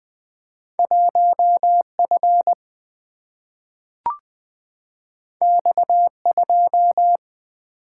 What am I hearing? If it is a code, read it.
Morse code, heard as a keyed tone, about 20 wpm, 700 Hz. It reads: 1F X2
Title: Single-Letter Single-Number 20wpm